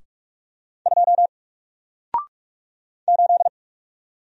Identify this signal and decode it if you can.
Morse code, heard as a keyed tone, about 44 wpm, 700 Hz. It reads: 2 8
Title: Single Character (Letter:Number:Punctuation:Prosign) 45wpm